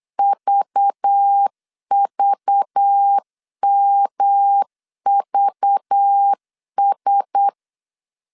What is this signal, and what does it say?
Morse code, heard as a keyed tone, about 8 wpm, 800 Hz. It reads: VVMVS
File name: CW / DET / 1 6 1500C S B 8wpm 5142